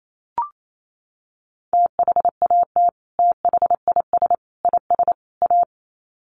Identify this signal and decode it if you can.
Morse code, heard as a keyed tone, about 28 wpm, 700 Hz. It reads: THAT THIS IS A
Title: Sets of 4 Words 28wpm